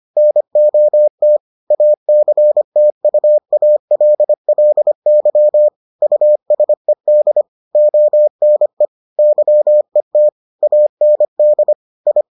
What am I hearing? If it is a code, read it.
Morse code, heard as a keyed tone, about 25 wpm, 600 Hz. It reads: NOT ACTUALLY USED ONE YET AND I